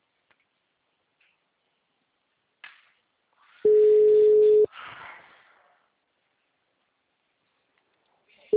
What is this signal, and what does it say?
Ringback tone (Europe, 425 Hz), with speech.
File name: out- -2017- - - .31087